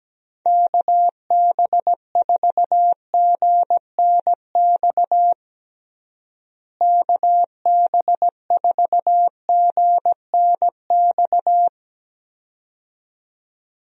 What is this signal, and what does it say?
Morse code, heard as a keyed tone, about 17 wpm, 700 Hz. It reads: KB4GNX KB4GNX
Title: USA Call Signs - 6 Characters - Repeated 2x - 17wpm